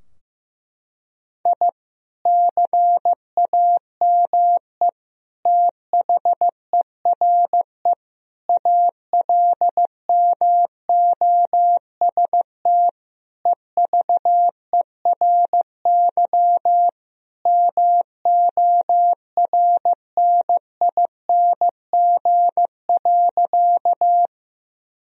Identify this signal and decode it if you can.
Morse code, heard as a keyed tone, about 15 wpm, 700 Hz. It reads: I CAME THERE ALMOST EVERY MORNING.
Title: Sentences from Top 1000 Words - Review Part 4 - 15wpm